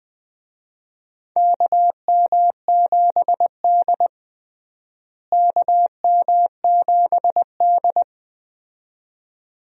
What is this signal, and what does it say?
Morse code, heard as a keyed tone, about 21 wpm, 700 Hz. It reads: KM7D KM7D
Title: USA Call Signs - 4 Characters - Repeated 2x - 20wpm